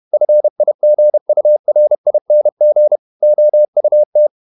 Morse code, heard as a keyed tone, about 31 wpm, 600 Hz. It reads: FIGURING OUT
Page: Never 31wpm | CW med Gnister